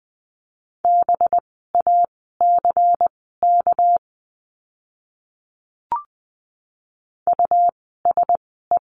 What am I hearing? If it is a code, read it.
Morse code, heard as a keyed tone, about 20 wpm, spaced as for 15 wpm, 700 Hz. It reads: BACK USE
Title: 230 Lesson 37f - Letter x - Words 20wpm